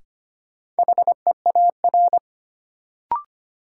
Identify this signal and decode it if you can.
Morse code, heard as a keyed tone, about 25 wpm, 700 Hz. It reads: HEAR
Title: Sentences from Top 300 Words - New Word Drill - 25wpm